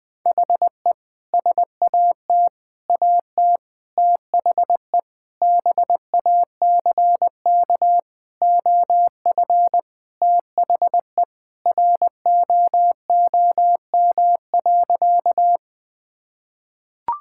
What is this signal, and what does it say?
Morse code, heard as a keyed tone, about 20 wpm, 700 Hz. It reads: HE SAT AT THE BACK OF THE ROOM.
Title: Sentences with 3-Word Phrases - 20wpm